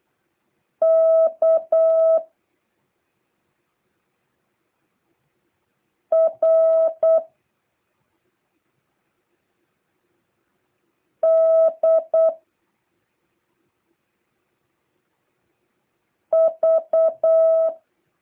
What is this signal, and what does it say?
Morse code, heard as a keyed tone, about 8 wpm, 1300 Hz. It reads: K R D V